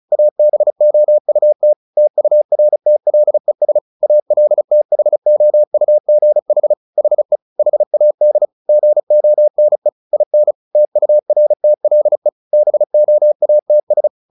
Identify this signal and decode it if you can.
Morse code, heard as a keyed tone, about 34 wpm, 600 Hz. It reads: ABOUT TURTLES ALTHOUGH HE HAD GONE IN TURTLE BOATS
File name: the old man and the sea / the old man and the sea 1 35wpm